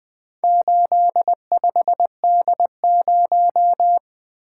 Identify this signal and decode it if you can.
Morse code, heard as a keyed tone, about 20 wpm, 700 Hz. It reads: 85D0
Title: Four Letter-Number Combinations - 20wpm